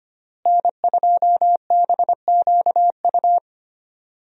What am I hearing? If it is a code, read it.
Morse code, heard as a keyed tone, about 25 wpm, 700 Hz. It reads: N2BQU
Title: USA Call Signs - 5 Characters - Repeated 3x - 25wpm